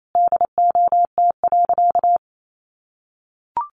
Morse code, heard as a keyed tone, about 28 wpm, 700 Hz. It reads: DOT.
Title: Sentences from Top 1500 Words 28wpm